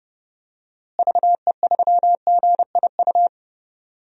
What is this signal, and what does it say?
Morse code, heard as a keyed tone, about 30 wpm, 700 Hz. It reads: VE3GIU
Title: DXCC-100 Call Signs - 6 Characters - Repeated 3x - 30wpm